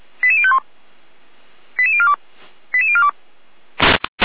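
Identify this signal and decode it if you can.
Five-tone selective call (ZVEI), heard as quick runs of five tones, tones of about 70 ms. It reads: 88051 88042 88042
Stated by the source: ZVEI five-tone call received by SDR.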